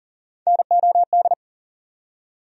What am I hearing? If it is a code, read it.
Morse code, heard as a keyed tone, about 40 wpm, 700 Hz. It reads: NOD
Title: USA Call Sign Postfixes - 3 Letters 40wpm